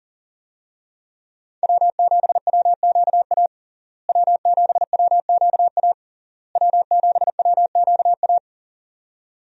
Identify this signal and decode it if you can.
Morse code, heard as a keyed tone, about 40 wpm, 700 Hz. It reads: W7WQA W7WQA W7WQA
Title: USA Call Signs - 5 Characters - Repeated 3x - 40wpm